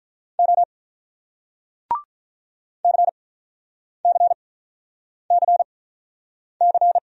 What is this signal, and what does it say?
Morse code, heard as a keyed tone, about 48 wpm, spaced as for 9 wpm, 700 Hz. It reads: O CCCC